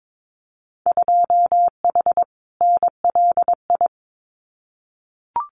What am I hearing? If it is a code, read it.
Morse code, heard as a keyed tone, about 22 wpm, 700 Hz. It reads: 2H NLI
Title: Winter Field Day Exchanges 22wpm